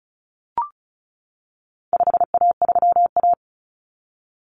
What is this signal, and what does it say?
Morse code, heard as a keyed tone, about 34 wpm, 700 Hz. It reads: 5A3A
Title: Common Cut Numbers - 4-Digit 35wpm